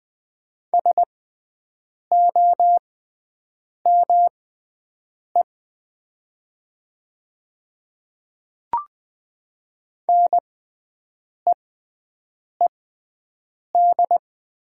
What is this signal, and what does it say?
Morse code, heard as a keyed tone, about 20 wpm, spaced as for 7 wpm, 700 Hz. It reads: SOME NEED
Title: 6x Character Spacing - Top 200 Words - 20wpm